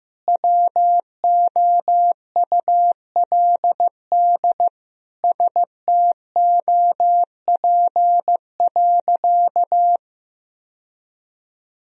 Morse code, heard as a keyed tone, about 16 wpm, 700 Hz. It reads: WOULD STOP.
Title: Sentences from Top 1000 Words - Review Part 1 - 15wpm